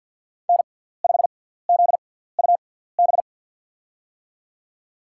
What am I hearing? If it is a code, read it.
Morse code, heard as a keyed tone, about 53 wpm, spaced as for 17 wpm, 700 Hz. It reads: N5ZUB
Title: USA Call Signs - 5 Characters - 6x Character Spacing - 50wpm